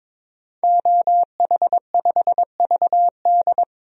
Morse code, heard as a keyed tone, about 22 wpm, 700 Hz. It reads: OH5VD
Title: DXCC-100 Call Signs - 5 Characters - 22wpm